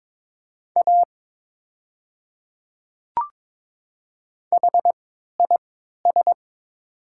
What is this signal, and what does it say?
Morse code, heard as a keyed tone, about 22 wpm, spaced as for 13 wpm, 700 Hz. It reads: A HIS
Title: 3x Character Spacing - Top 100 Words - 22wpm